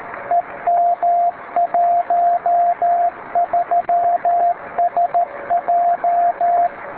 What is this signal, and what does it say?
Morse code, heard as a keyed tone, about 13 wpm, 700 Hz, with radio noise. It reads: EM13SJ
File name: beacons / w5hn-b